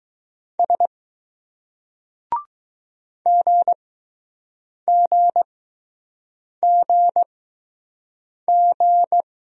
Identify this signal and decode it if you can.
Morse code, heard as a keyed tone, about 20 wpm, spaced as for 6 wpm, 700 Hz. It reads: S GGGG